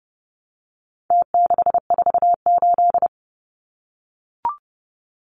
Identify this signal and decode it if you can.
Morse code, heard as a keyed tone, about 30 wpm, 700 Hz. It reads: T648
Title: Numbers - Contest Style - 4-Digit 30wpm